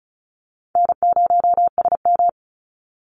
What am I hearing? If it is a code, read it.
Morse code, heard as a keyed tone, about 34 wpm, 700 Hz. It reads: N0SM
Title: Call Signs - 35wpm